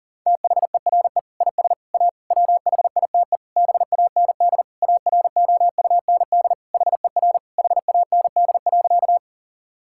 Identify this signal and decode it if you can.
Morse code, heard as a keyed tone, about 40 wpm, 700 Hz. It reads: THERE IS A WHITE BAND AROUND HER HAND.
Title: Sentences from Top 1600 Words 40wpm